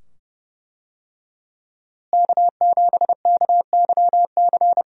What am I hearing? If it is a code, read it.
Morse code, heard as a keyed tone, about 29 wpm, 700 Hz. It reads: K7KYC
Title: USA Call Signs - 5 Characters - Repeated 2x - 30wpm